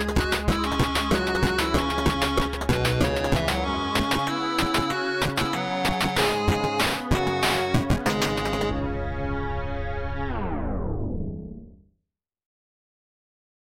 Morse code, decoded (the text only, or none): none